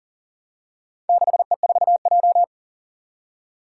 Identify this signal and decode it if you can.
Morse code, heard as a keyed tone, about 42 wpm, 700 Hz. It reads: 6E4J
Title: Four Challenging Letter-Number Combinations 40wpm